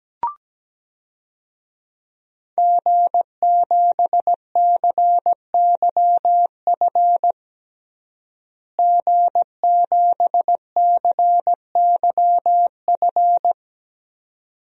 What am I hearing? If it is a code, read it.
Morse code, heard as a keyed tone, about 17 wpm, 700 Hz. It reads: G7CYF G7CYF
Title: DXCC-100 Call Signs - 5 Characters - Repeated 2x - 17wpm